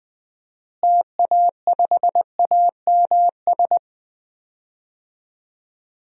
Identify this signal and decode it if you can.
Morse code, heard as a keyed tone, about 20 wpm, 700 Hz. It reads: TA5AMS
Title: DXCC-100 Call Signs - 6 Characters - Repeated 3x - 20wpm